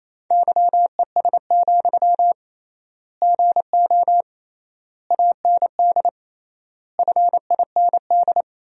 Morse code, heard as a keyed tone, about 28 wpm, 700 Hz. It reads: YES, GO AND FIND
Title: 3x Word-Spacing - Sentences from Top 500 Words - 28wpm